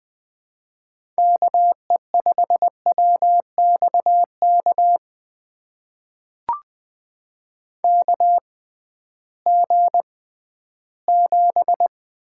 Morse code, heard as a keyed tone, about 20 wpm, spaced as for 13 wpm, 700 Hz. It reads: KE5WXK KG7
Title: USA Call Signs - 6 Characters - 6x Character Spacing - 20wpm